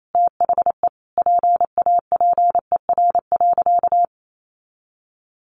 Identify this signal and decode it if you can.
Morse code, heard as a keyed tone, about 28 wpm, 700 Hz. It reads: THE PAPER.
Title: Sentences from Top 1400 Words 28wpm